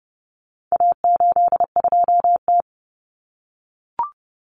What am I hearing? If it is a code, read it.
Morse code, heard as a keyed tone, about 29 wpm, 700 Hz. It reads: A82T
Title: Common Cut Numbers - 4-Digit 30wpm